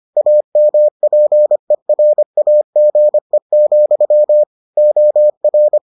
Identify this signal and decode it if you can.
Morse code, heard as a keyed tone, about 25 wpm, 600 Hz. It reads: AMPERAGE, OR